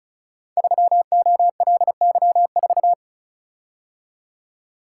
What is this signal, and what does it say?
Morse code, heard as a keyed tone, about 36 wpm, 690 Hz. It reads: 3OLY4
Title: Code Groups - 5 Letter:Numbers - 35wpm